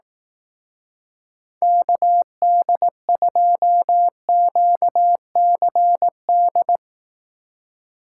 Morse code, heard as a keyed tone, about 18 wpm, 700 Hz. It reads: KD2QCD
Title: Speed Racing - USA 2x3 Call Signs 15wpm